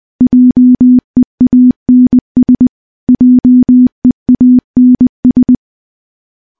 Morse code, heard as a keyed tone, about 20 wpm, 250 Hz. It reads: JEANS JEANS
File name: morse-training-20WPM